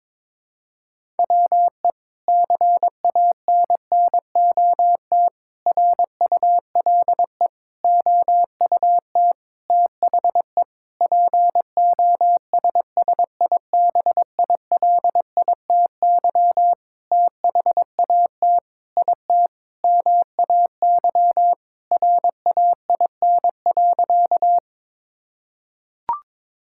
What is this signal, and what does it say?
Morse code, heard as a keyed tone, about 22 wpm, 700 Hz. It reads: WE CANNOT RULE OUT THE POSSIBILITY THAT IT MAY RAIN.
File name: practice / 22wpm / Sentences with 8-Word Phrases - 22wpm